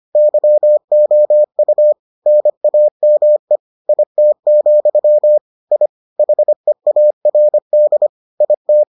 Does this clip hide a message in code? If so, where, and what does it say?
Yes: Morse code from 0.2 to 8.8 s, heard as a keyed tone, about 25 wpm, 600 Hz. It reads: YOU NAME IT, I HEARD IT